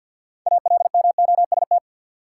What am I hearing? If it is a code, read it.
Morse code, heard as a keyed tone, about 52 wpm, 700 Hz. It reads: ALMOST
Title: Sentences from Top 700 Words - New Word Drill - 50wpm